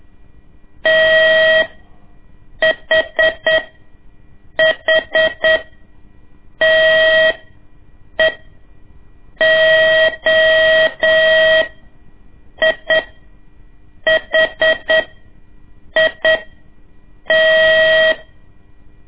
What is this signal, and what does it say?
Morse code, heard as a keyed tone, about 8 wpm, spaced as for 6 wpm, 640 Hz. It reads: THHTE OIHIT
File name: ik1weg2 / CORSI CW / lez2-1